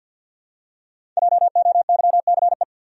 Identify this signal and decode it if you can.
Morse code, heard as a keyed tone, about 47 wpm, 700 Hz. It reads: JOYCE
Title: USA Common Names - 50wpm